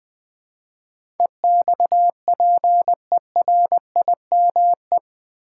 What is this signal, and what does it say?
Morse code, heard as a keyed tone, about 20 wpm, 700 Hz. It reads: EXPERIME
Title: Build A Word - 4 Letters At A Time - 10-Letter Words - 20wpm